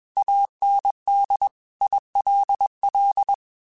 Morse code, heard as a keyed tone, about 21 wpm, 800 Hz. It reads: AND ILL
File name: QSOs / QSOs-mp3s / kc7haa@20